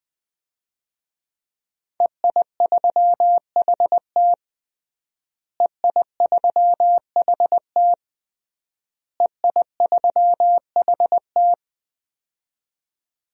Morse code, heard as a keyed tone, about 20 wpm, 700 Hz. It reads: EI3HT EI3HT EI3HT
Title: DXCC-100 Call Signs - 5 Characters - Repeated 3x - 20wpm